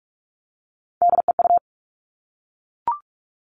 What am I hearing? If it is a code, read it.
Morse code, heard as a keyed tone, about 44 wpm, 700 Hz. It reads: DEU
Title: All Cut Numbers - 3-Digit 45wpm